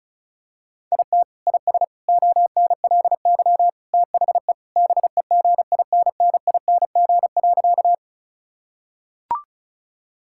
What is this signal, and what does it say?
Morse code, heard as a keyed tone, about 34 wpm, 700 Hz. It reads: IT IS ONLY THE BEGINNING.
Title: Sentences from Top 1000 Words - Review Part 5 - 35wpm